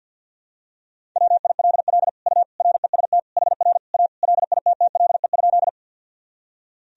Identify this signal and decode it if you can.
Morse code, heard as a keyed tone, about 52 wpm, 700 Hz. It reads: WILL U REST HR A LITTLE?
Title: Sentences from Top 500 Words and Abbreviations 50wpm